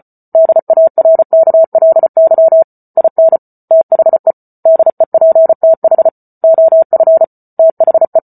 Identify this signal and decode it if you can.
Morse code, heard as a keyed tone, about 34 wpm, 650 Hz. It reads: DARKLY IN THE DEPTH OF THE